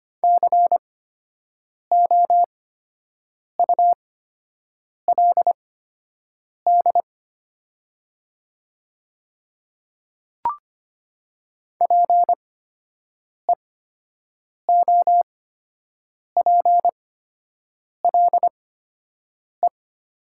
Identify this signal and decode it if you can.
Morse code, heard as a keyed tone, about 25 wpm, spaced as for 7 wpm, 700 Hz. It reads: COULD PEOPLE
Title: 8x Character Spacing - Top 100 Words - 25wpm